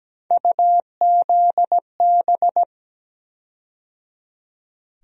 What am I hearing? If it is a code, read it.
Morse code, heard as a keyed tone, about 17 wpm, 700 Hz. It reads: UZB
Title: USA Call Sign Postfixes - 3 Letters 17wpm